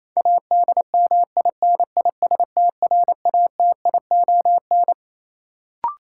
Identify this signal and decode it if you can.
Morse code, heard as a keyed tone, about 28 wpm, 700 Hz. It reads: ADMINISTRATION
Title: Sets of 5 Words - New Word Drill - 28wpm